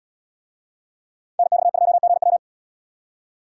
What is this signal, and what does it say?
Morse code, heard as a keyed tone, about 77 wpm, 700 Hz. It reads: NX2GW